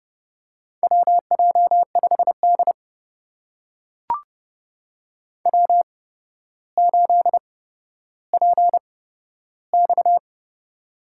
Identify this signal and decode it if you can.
Morse code, heard as a keyed tone, about 31 wpm, spaced as for 13 wpm, 700 Hz. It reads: WJ5D W8PX
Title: USA Call Signs - 4 Characters - 8x Character Spacing - 30wpm